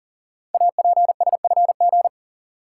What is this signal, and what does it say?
Morse code, heard as a keyed tone, about 38 wpm, 700 Hz. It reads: APSFG
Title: Code Groups - 5 Letters - 40wpm